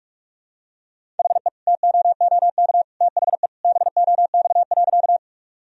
Morse code, heard as a keyed tone, about 46 wpm, 700 Hz. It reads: HE TOOK THE BOX.
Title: Sentences from Top 1000 Words - Review Part 2 - 45wpm